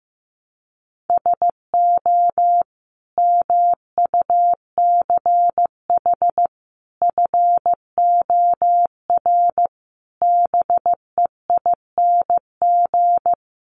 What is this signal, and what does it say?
Morse code, heard as a keyed tone, about 14 wpm, 700 Hz. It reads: SO MUCH FOR BEING
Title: Sets of 5 Words Encore 15wpm